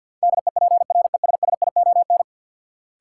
Morse code, heard as a keyed tone, about 52 wpm, 700 Hz. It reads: DEPRESSION
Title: Sentences from Top 1300 Words - New Word Drill Alt - 50wpm